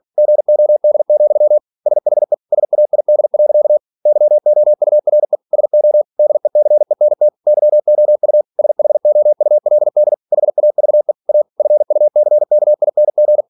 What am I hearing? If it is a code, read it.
Morse code, heard as a keyed tone, about 46 wpm, 600 Hz. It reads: GOD, SHE SAID. YOURE SO DECENT YOU SHOULD HAVE A FUCKING
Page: Never 47wpm